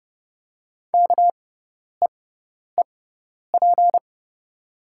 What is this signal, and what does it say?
Morse code, heard as a keyed tone, about 30 wpm, spaced as for 10 wpm, 700 Hz. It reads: KEEP
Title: 6x Character Spacing - Top 300 Words - 30wpm